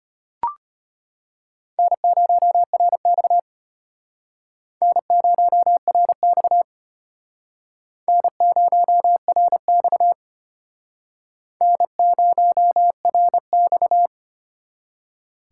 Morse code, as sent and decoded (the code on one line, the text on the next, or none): -. ----- .-. -..- / -. ----- .-. -..- / -. ----- .-. -..- / -. ----- .-. -..-
N0RX N0RX N0RX N0RX